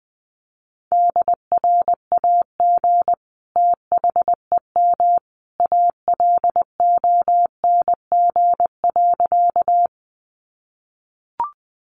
Morse code, heard as a keyed tone, about 20 wpm, 700 Hz. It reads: DRAG THEM ALONG.
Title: Sentences from Top 1500 Words 20wpm